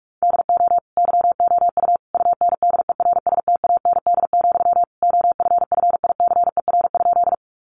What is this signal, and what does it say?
Morse code, heard as a keyed tone, about 44 wpm, 700 Hz. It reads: DO YOU UNDERSTAND, OFFICER?
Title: Sentences from Top 1200 Words 45wpm